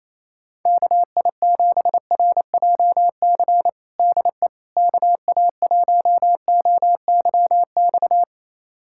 Morse code, heard as a keyed tone, about 27 wpm, 700 Hz. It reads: KI7RJC DE KA1OYX
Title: USA Callsign DE Callsign - 6 Characters - 28wpm